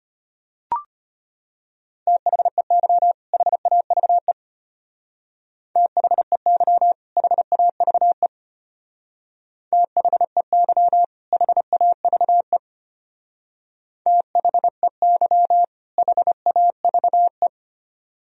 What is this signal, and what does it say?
Morse code, heard as a keyed tone, about 30 wpm, 700 Hz. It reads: THEY HAVE THEY HAVE THEY HAVE THEY HAVE
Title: Speed-Racing - Sets of 2 Words 25wpm